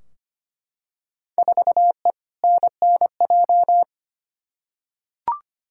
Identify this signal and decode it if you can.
Morse code, heard as a keyed tone, about 25 wpm, 700 Hz. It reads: 4E NNJ
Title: Field Day Exchanges - 25wpm